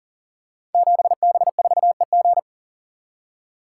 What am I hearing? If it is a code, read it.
Morse code, heard as a keyed tone, about 40 wpm, 700 Hz. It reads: 7B4EG
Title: Code Groups - 5 Letter:Numbers - 40wpm